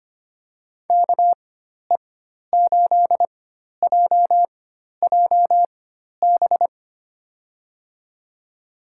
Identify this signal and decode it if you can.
Morse code, heard as a keyed tone, about 25 wpm, spaced as for 12 wpm, 700 Hz. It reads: KE8JJB
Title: USA Call Signs - 6 Characters - 4x Character Spacing - 25wpm